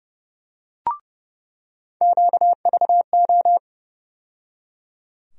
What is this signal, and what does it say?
Morse code, heard as a keyed tone, about 30 wpm, 700 Hz. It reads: QVO
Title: USA Call Sign Postfixes - 3 Letters 30wpm